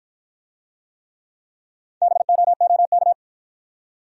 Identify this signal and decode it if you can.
Morse code, heard as a keyed tone, about 53 wpm, 700 Hz. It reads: BOOK